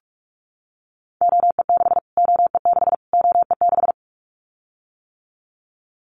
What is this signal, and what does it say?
Morse code, heard as a keyed tone, about 44 wpm, 700 Hz. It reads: OE6 OE6 OE6
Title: Callsign Prefixes for Top 25 DXCC - Repeated 3x - 45wpm